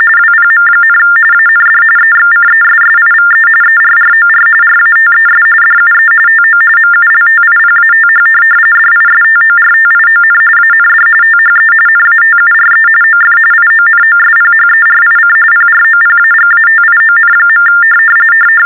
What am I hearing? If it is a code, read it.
TTY/TDD text telephone tones, about 45.45 baud. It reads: WPM, AND EXCESSIVE ERROR RATES DUE TO MULTIPATH DISTORTION AND THE NATURE OF IONOSPHERIC PROPAGATION.
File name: teletype-RTTY-45